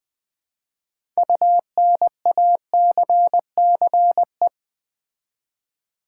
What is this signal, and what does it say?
Morse code, heard as a keyed tone, about 20 wpm, 700 Hz. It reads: UNACCE
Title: Build A Word - 6 Letters At A Time - 12-Letter Words - 20wpm